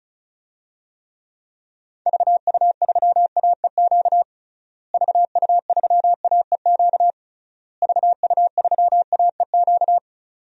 Morse code, heard as a keyed tone, about 36 wpm, 700 Hz. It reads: VU3AEQ VU3AEQ VU3AEQ
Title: DXCC-100 Call Signs - 6 Characters - Repeated 3x - 35wpm